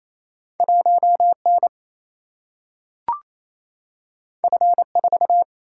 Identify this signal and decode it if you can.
Morse code, heard as a keyed tone, about 28 wpm, 700 Hz. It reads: 1 N F 4
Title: Single-Letter Single-Number 28wpm